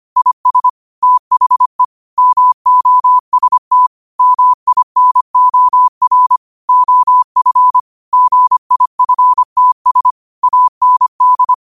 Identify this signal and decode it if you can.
Morse code, heard as a keyed tone, about 25 wpm, 1000 Hz. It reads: IS THE MOST MINOR OF GIFTS AND
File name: qotdfiles / 25wpm